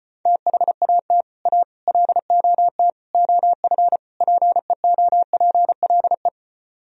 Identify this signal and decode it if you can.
Morse code, heard as a keyed tone, about 33 wpm, 700 Hz. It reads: THAT A LOT OF PEOPLE